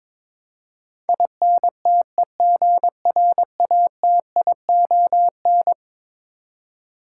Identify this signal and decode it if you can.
Morse code, heard as a keyed tone, about 22 wpm, 700 Hz. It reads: INTEGRATION
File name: practice / 22wpm / Eleven Letter Words - 22wpm